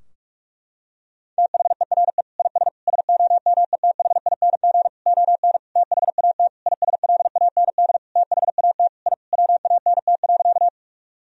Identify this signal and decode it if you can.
Morse code, heard as a keyed tone, about 46 wpm, 700 Hz. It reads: THERE IS SOMETHING ON THAT ISLAND THAT I WANT.
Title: Sentences from Top 1000 Words - Review Part 3 - 45wpm